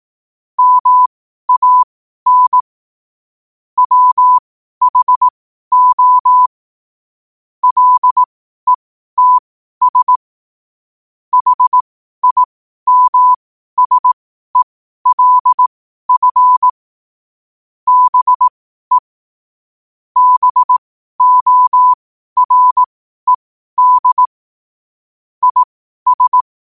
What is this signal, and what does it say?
Morse code, heard as a keyed tone, about 18 wpm, spaced as for 12 wpm, 1000 Hz. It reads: MAN WHO LETS HIMSELF BE BORED IS
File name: qotdfiles / 12wpm